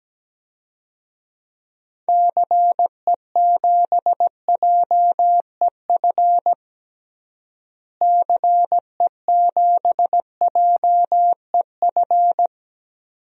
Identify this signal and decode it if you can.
Morse code, heard as a keyed tone, about 17 wpm, 700 Hz. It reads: CE7JEF CE7JEF
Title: DXCC-100 Call Signs - 6 Characters - Repeated 2x - 17wpm